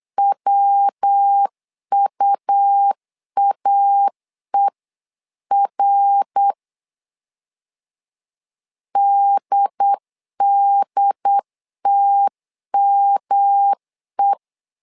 Morse code, heard as a keyed tone, about 8 wpm, 800 Hz. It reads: WUAER DDTME
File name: LEZIONI / 3 MWRO / 3 MWR 8wpm 2444